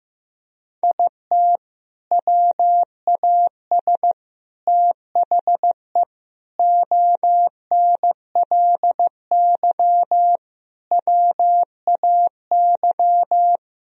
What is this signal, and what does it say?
Morse code, heard as a keyed tone, about 15 wpm, 700 Hz. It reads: IT WAS THE ONLY WAY
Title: Sentences from Top 1000 Words - Review Part 3 - 15wpm